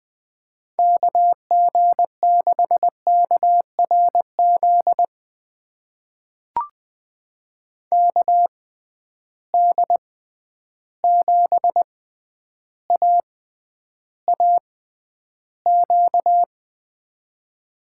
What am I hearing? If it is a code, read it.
Morse code, heard as a keyed tone, about 20 wpm, spaced as for 10 wpm, 700 Hz. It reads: KG6KRZ KD7AAQ
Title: USA Call Signs - 6 Characters - 6x Character Spacing - 20wpm